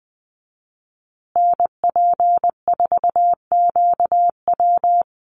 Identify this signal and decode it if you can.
Morse code, heard as a keyed tone, about 20 wpm, 700 Hz. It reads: NP4QW
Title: USA Call Signs Repeated 3x 20wpm